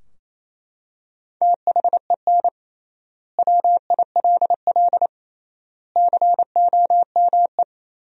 Morse code, heard as a keyed tone, about 28 wpm, 700 Hz. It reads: THEN WILL COME
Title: 3x Word-Spacing - Sentences from Top 200 Words - 28wpm